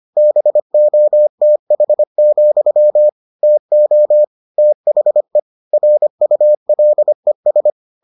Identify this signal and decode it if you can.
Morse code, heard as a keyed tone, about 25 wpm, 600 Hz. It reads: BOTH, TO THE RULES